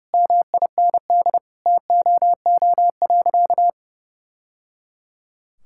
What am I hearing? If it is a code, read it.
Morse code, heard as a keyed tone, about 30 wpm, 700 Hz. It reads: MIND TOO.
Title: Sentences from Top 1000 Words - Review Part 6 - 30wpm